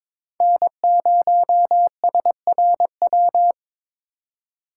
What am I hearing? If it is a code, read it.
Morse code, heard as a keyed tone, about 22 wpm, 700 Hz. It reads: N0SRW
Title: USA Call Signs - 3x Character Spacing - 22wpm